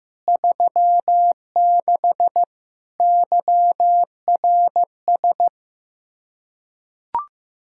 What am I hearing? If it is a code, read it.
Morse code, heard as a keyed tone, about 15 wpm, 700 Hz. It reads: 36 YRS
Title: Practice QSO - Exchange 3 - More Info - 15wpm